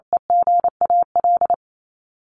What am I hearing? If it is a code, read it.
Morse code, heard as a keyed tone, about 28 wpm, 700 Hz. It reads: EGAL
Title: Top 1751-2000 Words 28wpm